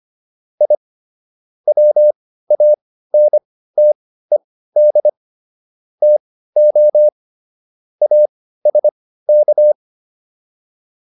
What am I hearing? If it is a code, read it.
Morse code, heard as a keyed tone, about 25 wpm, spaced as for 15 wpm, 600 Hz. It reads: I WANTED TO ASK